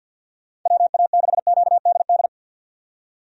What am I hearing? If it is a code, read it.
Morse code, heard as a keyed tone, about 51 wpm, 700 Hz. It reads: WA6QDD
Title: USA Call Signs - 6 Characters - 50wpm